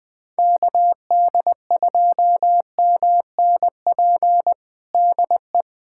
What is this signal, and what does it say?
Morse code, heard as a keyed tone, about 20 wpm, 700 Hz. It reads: KD2MNP DE
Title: USA Callsign DE Callsign 20wpm